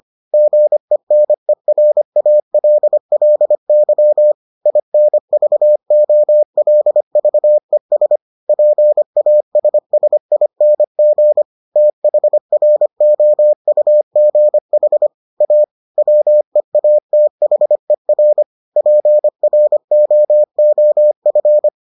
Morse code, heard as a keyed tone, about 25 wpm, 600 Hz. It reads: GENERALLY INVOLVES PASSING THROUGH A WEATHER PROOF